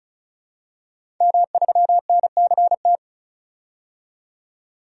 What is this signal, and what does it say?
Morse code, heard as a keyed tone, about 36 wpm, 700 Hz. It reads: M3NCT
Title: DXCC-100 Call Signs - 5 Characters - 35wpm